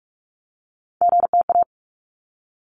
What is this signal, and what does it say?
Morse code, heard as a keyed tone, about 46 wpm, 700 Hz. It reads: GTA
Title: ARRL Sections 45wpm